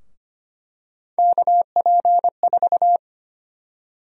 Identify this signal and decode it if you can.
Morse code, heard as a keyed tone, about 25 wpm, 700 Hz. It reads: KP4
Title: USA Call Sign Prefixes - Repeated 2x - 25wpm